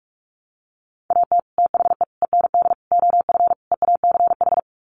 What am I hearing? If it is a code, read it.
Morse code, heard as a keyed tone, about 44 wpm, 700 Hz. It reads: AT THE END OF EACH
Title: Sets of 5 Words Encore 45wpm